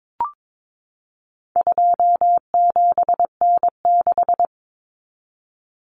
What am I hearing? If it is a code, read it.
Morse code, heard as a keyed tone, about 22 wpm, 700 Hz. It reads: 27N6
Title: Common Cut Numbers - 4-Digit 22wpm